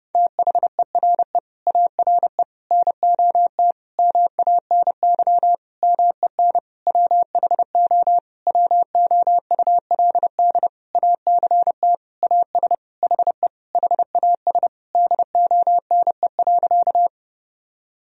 Morse code, heard as a keyed tone, about 30 wpm, 700 Hz. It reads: THERE ARE NOT MANY MEN WHO WOULD ACT AS HE HAS DONE.
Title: Sentences from Top 1000 Words - Review Part 1 - 30wpm